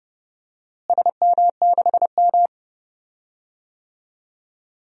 Morse code, heard as a keyed tone, about 30 wpm, 700 Hz. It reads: SM6M
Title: DXCC-100 Call Signs - 4 Characters - Repeated 2x - 30wpm